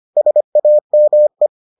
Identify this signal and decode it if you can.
Morse code, heard as a keyed tone, about 25 wpm, 600 Hz. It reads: SAME